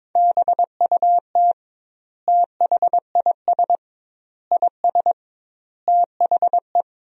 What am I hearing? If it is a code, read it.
Morse code, heard as a keyed tone, about 22 wpm, 700 Hz. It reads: BUT THIS IS THE
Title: 2x Word-Spacing - Sentences from Top 200 Words - 22wpm